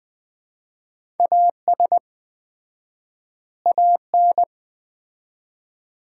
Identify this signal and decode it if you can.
Morse code, heard as a keyed tone, about 20 wpm, 700 Hz. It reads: AS AN
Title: 4x Word-Spacing - Sets of 2 Words - 20wpm